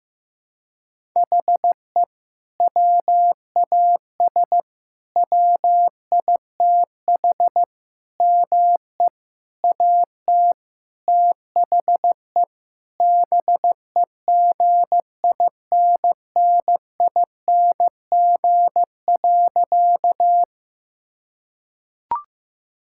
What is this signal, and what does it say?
Morse code, heard as a keyed tone, about 15 wpm, 700 Hz. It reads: HE WAS WITH ME AT THE BEGINNING.
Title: Sentences from Top 1000 Words - Review Part 5 - 15wpm